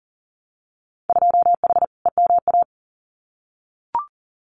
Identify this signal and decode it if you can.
Morse code, heard as a keyed tone, about 40 wpm, 700 Hz. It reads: 2H EMA
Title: Winter Field Day Exchanges 40wpm